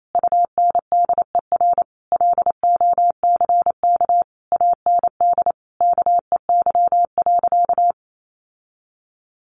Morse code, heard as a keyed tone, about 28 wpm, 700 Hz. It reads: UNDER LOCK AND KEY.
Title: Sentences from Top 1400 Words 28wpm